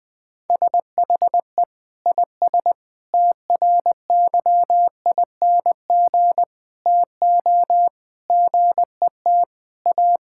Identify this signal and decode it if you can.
Morse code, heard as a keyed tone, about 20 wpm, 700 Hz. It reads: SHE IS TRYING TO GET A
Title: Sentences with 3-Word Phrases - 20wpm